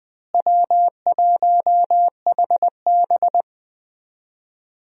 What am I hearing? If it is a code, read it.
Morse code, heard as a keyed tone, about 20 wpm, 700 Hz. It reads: W1HB
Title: USA Call Signs - 4 Characters - Repeated 3x - 20wpm